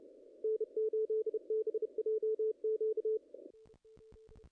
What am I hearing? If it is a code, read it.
Morse code, heard as a keyed tone, about 36 wpm, 440 Hz, with radio noise. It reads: N8BJQ
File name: contests / ss-cw-k5tr-op-n5rz -0000